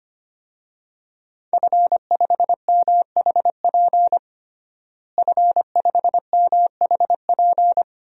Morse code, heard as a keyed tone, about 25 wpm, 700 Hz. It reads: F5MHP F5MHP
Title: DXCC-100 Call Signs - 5 Characters - Repeated 2x - 25wpm